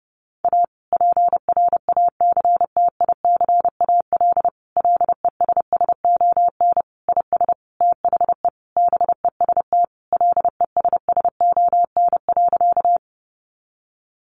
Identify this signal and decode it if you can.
Morse code, heard as a keyed tone, about 30 wpm, 700 Hz. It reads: A PRACTICAL LESSON IS THE BEST LESSON.
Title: Sentences from Top 1400 Words 30wpm